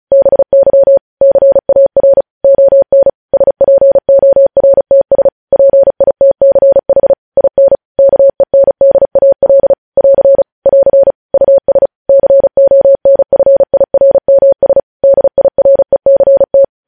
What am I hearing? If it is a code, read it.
Morse code, heard as a keyed tone, about 36 wpm, 550 Hz. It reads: BY CAR ON SPORTS PITCH IN KENDAL <AR> <AR> US CONFIRMS DIRECT
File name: podcast / News headlines-1x-35x35